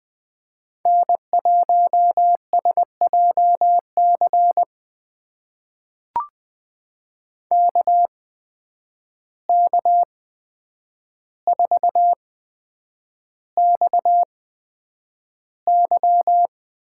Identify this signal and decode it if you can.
Morse code, heard as a keyed tone, about 20 wpm, spaced as for 9 wpm, 700 Hz. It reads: N1SJC KK4XY
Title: USA Call Signs - 5 Characters - 8x Character Spacing - 20wpm